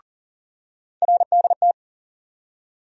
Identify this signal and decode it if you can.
Morse code, heard as a keyed tone, about 40 wpm, 700 Hz. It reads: RDT
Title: USA Call Sign Postfixes - 3 Letters 40wpm